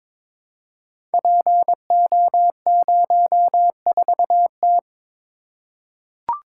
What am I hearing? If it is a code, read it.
Morse code, heard as a keyed tone, about 22 wpm, 700 Hz. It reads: PO04T
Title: Code Groups - 5 Letter:Numbers - 22wpm